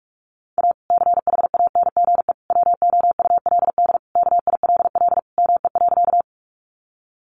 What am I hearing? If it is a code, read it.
Morse code, heard as a keyed tone, about 44 wpm, 700 Hz. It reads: A CHANGE WOULD KILL ME.
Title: Sentences from Top 1400 Words 45wpm